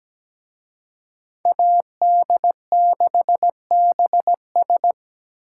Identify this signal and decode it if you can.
Morse code, heard as a keyed tone, about 20 wpm, 700 Hz. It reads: AD6BS
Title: Speed Racing - USA Call Signs 17wpm